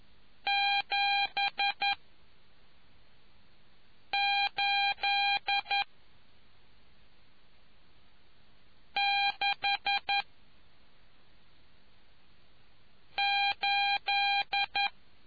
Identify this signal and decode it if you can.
Morse code, heard as a keyed tone, about 11 wpm, 800 Hz. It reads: 7 8 6 8